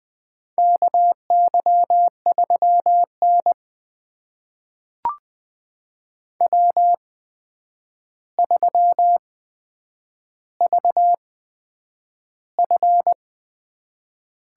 Morse code, heard as a keyed tone, about 20 wpm, spaced as for 9 wpm, 700 Hz. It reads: KY3N W3VF
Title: USA Call Signs - 8x Character Spacing - 20wpm